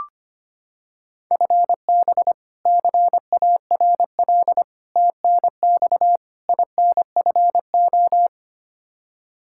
Morse code, heard as a keyed tone, about 25 wpm, 700 Hz. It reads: FB CARL TNX INFO
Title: Practice QSO - Exchange 3 - More Info - 25wpm